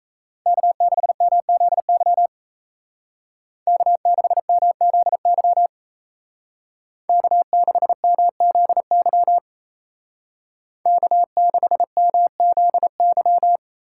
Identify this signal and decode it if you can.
Morse code, heard as a keyed tone, about 33 wpm, 700 Hz. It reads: K6MZY K6MZY K6MZY K6MZY
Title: Speed Racing - USA 1x3 Call Signs 28wpm